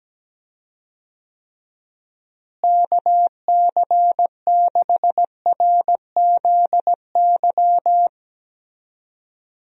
Morse code, heard as a keyed tone, about 17 wpm, 700 Hz. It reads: KC6RZY
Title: USA Call Signs - 6 Characters - Repeated 3x - 17wpm